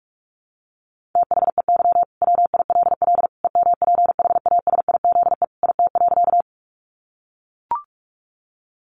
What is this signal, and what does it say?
Morse code, heard as a keyed tone, about 44 wpm, 700 Hz. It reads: THEY WILL EMPHASIZE IT.
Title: Sentences from Top 1500 Words 45wpm